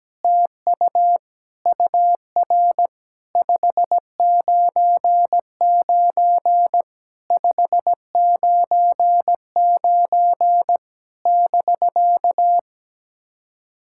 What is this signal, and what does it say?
Morse code, heard as a keyed tone, about 17 wpm, 700 Hz. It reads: TU UR 599 599 <BK>
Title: SOTA QSO Practice 17wpm